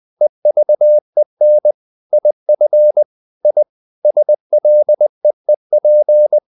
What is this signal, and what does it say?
Morse code, heard as a keyed tone, about 20 wpm, 600 Hz. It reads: EVEN IF I SLEEP
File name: the old man and the sea part 2 20wpm